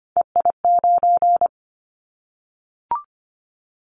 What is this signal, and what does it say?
Morse code, heard as a keyed tone, about 25 wpm, 700 Hz. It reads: EI9
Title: Callsign Prefixes for Top 50 DXCC - 25wpm